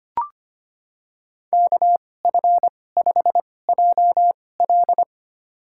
Morse code, heard as a keyed tone, about 25 wpm, spaced as for 18 wpm, 700 Hz. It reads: KF5JL
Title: USA Call Signs - 5 Characters - 2x Character Spacing - 25wpm